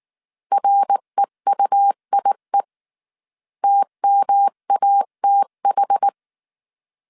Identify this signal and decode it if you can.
Morse code, heard as a keyed tone, about 19 wpm, 800 Hz. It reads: REUIE TMATH
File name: CW / DET / 1 3 1500C S B 18wpm 1916